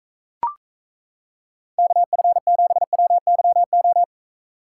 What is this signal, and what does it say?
Morse code, heard as a keyed tone, about 35 wpm, 700 Hz. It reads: KF7WYO
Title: Speed Racing - USA 2x3 Call Signs 28wpm